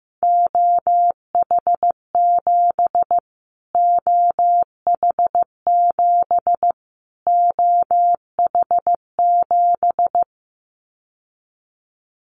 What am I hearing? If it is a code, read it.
Morse code, heard as a keyed tone, about 15 wpm, 700 Hz. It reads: OH7 OH7 OH7
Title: Callsign Prefixes for Top 25 DXCC - Repeated 3x - 15wpm